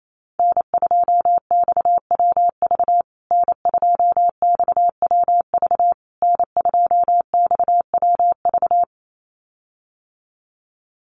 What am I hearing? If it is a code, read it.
Morse code, heard as a keyed tone, about 28 wpm, 700 Hz. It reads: N2XWV N2XWV N2XWV
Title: USA Call Signs Repeated 3x 28wpm